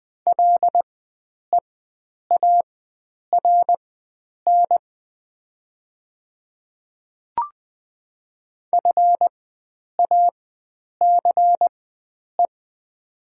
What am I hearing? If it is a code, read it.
Morse code, heard as a keyed tone, about 20 wpm, spaced as for 9 wpm, 700 Hz. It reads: LEARN FACE
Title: 4x Character Spacing - Top 300 Words - 20wpm